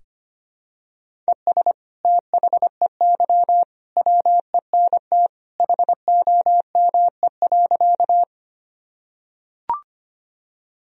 Morse code, heard as a keyed tone, about 25 wpm, 700 Hz. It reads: ES THEY WENT HOME.
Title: Sentences from Top 200 Words and Abbreviations 25wpm